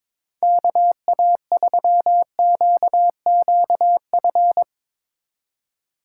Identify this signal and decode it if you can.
Morse code, heard as a keyed tone, about 24 wpm, 700 Hz. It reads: KA3QQF
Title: Speed Racing - USA 2x3 Call Signs 22wpm